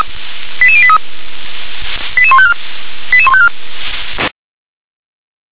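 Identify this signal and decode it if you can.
Five-tone selective call (ZVEI), heard as quick runs of five tones, tones of about 70 ms. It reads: 88083 88154 88154
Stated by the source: ZVEI five-tone call received by SDR.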